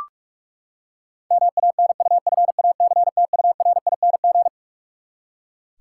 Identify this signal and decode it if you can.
Morse code, heard as a keyed tone, about 44 wpm, 700 Hz. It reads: MANUFACTURING
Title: Practice QSO - Exchange 3 - More Info - 45wpm